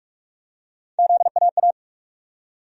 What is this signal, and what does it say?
Morse code, heard as a keyed tone, about 46 wpm, 700 Hz. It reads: ZAA
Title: USA Call Sign Postfixes - 3 Letters 45wpm